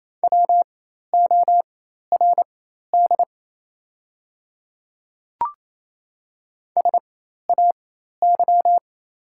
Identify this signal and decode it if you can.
Morse code, heard as a keyed tone, about 27 wpm, spaced as for 13 wpm, 700 Hz. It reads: WORD SAY
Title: 4x Character Spacing - Top 300 Words - 28wpm